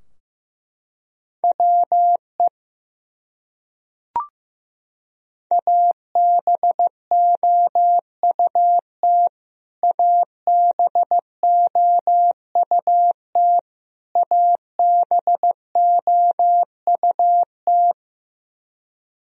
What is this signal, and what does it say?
Morse code, heard as a keyed tone, about 16 wpm, 700 Hz. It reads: WE ABOUT ABOUT ABOUT
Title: Top 100 Words Repeated 15wpm